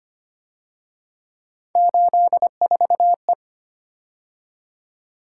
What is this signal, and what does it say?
Morse code, heard as a keyed tone, about 30 wpm, 700 Hz. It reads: 84E
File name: Speed-Racing - Three Letter-Number Combinations - 25wpm